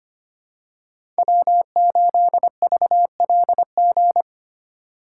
Morse code, heard as a keyed tone, about 25 wpm, 700 Hz. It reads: W8VLG
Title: USA Call Signs - 5 Characters - 3x Character Spacing - 25wpm